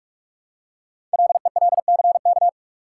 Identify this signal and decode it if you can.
Morse code, heard as a keyed tone, about 44 wpm, 690 Hz. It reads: LELCK
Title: Code Groups - 5 Letters - 45wpm